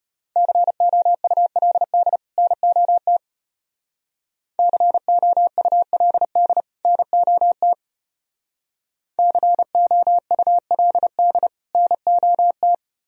Morse code, heard as a keyed tone, about 31 wpm, 700 Hz. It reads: COULD NOT COULD NOT COULD NOT
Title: Speed-Racing - Sets of 2 Words 25wpm